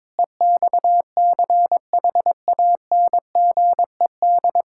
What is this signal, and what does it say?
Morse code, heard as a keyed tone, about 22 wpm, 700 Hz. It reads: EXCHANGED
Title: Pangram Challenge v2 - 22wpm